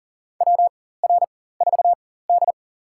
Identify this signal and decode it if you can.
Morse code, heard as a keyed tone, about 40 wpm, spaced as for 19 wpm, 700 Hz. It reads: WR4D
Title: USA Call Signs - 4 Characters - 4x Character Spacing - 40wpm